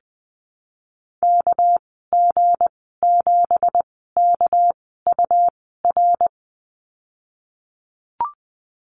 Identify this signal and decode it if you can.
Morse code, heard as a keyed tone, about 20 wpm, spaced as for 15 wpm, 700 Hz. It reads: KG7KUR
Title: 200 Lesson 32g - Letter k - Call Signs 20wpm